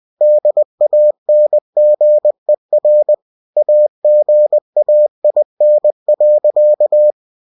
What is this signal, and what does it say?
Morse code, heard as a keyed tone, about 20 wpm, 600 Hz. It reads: DANGER AGAIN.
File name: cw-books / winnie-the-pooh / winnie-20wpm-page-0024